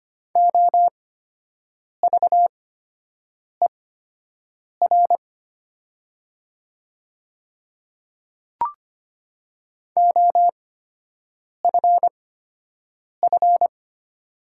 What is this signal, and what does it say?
Morse code, heard as a keyed tone, about 25 wpm, spaced as for 7 wpm, 700 Hz. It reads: OVER OFF
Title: 8x Character Spacing - Top 200 Words - 25wpm